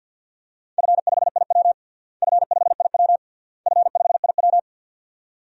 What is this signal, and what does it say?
Morse code, heard as a keyed tone, about 53 wpm, 700 Hz. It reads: F5IW F5IW F5IW
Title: DXCC-100 Call Signs - 4 Characters - Repeated 3x - 50wpm